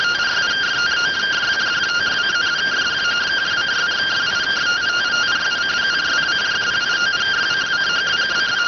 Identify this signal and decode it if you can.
Two-tone teleprinter signal, about 170 Hz shift.